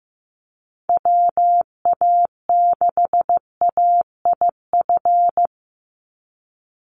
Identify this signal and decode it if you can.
Morse code, heard as a keyed tone, about 15 wpm, 700 Hz. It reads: WA6AIF
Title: USA Call Signs 15wpm